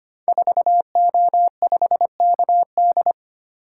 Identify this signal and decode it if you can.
Morse code, heard as a keyed tone, about 25 wpm, 700 Hz. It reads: 4O5KD
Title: Code Groups - 5 Letter:Numbers - 25wpm